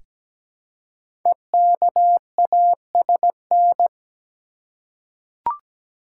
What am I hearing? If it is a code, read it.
Morse code, heard as a keyed tone, about 17 wpm, 700 Hz. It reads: EKASN
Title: Code Groups - 5 Letters - 17wpm